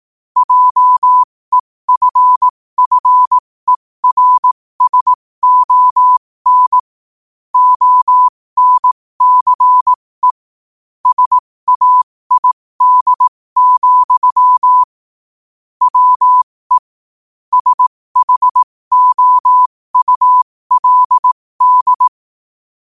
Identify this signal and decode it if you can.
Morse code, heard as a keyed tone, about 18 wpm, spaced as for 15 wpm, 1000 Hz. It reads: JEFFERSON ONCE SAID, WE SHOULD
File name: qotdfiles / 15wpm